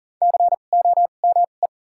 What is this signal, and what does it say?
Morse code, heard as a keyed tone, about 40 wpm, spaced as for 29 wpm, 700 Hz. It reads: COME
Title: 2x Character Spacing - Top 100 Words - 40wpm